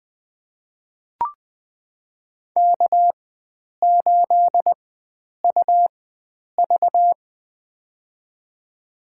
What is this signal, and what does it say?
Morse code, heard as a keyed tone, about 20 wpm, spaced as for 9 wpm, 700 Hz. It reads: K8UV
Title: USA Call Signs - 4 Characters - 4x Character Spacing - 20wpm